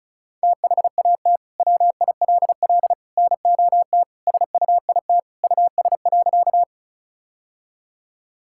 Morse code, heard as a keyed tone, about 34 wpm, 700 Hz. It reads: THAT WILL NOT SUIT US.
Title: Sentences from Top 1600 Words 35wpm